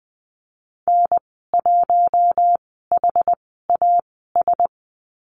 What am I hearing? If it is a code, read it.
Morse code, heard as a keyed tone, about 20 wpm, spaced as for 14 wpm, 700 Hz. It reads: N1HAS
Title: 069 Lesson 12g - Letter d - Call Signs 20wpm